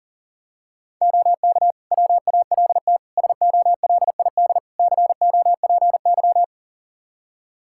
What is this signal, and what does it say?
Morse code, heard as a keyed tone, about 40 wpm, 700 Hz. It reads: OK WALT SOLID COPY
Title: Practice QSO - Exchange 3 - More Info - 40wpm